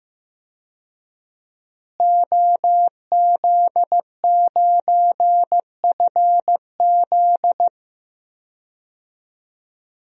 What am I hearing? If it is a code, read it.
Morse code, heard as a keyed tone, about 15 wpm, 700 Hz. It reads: OZ9FZ
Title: DXCC-100 Call Signs - 5 Characters - Repeated 3x - 15wpm